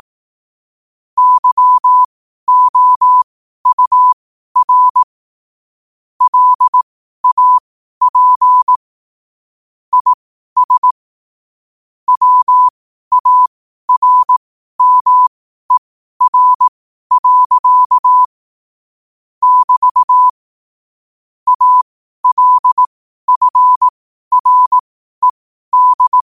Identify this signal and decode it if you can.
Morse code, heard as a keyed tone, about 18 wpm, spaced as for 12 wpm, 1000 Hz. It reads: YOUR LAP IS WARMER. <BT> ALFRED
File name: qotdfiles / 12wpm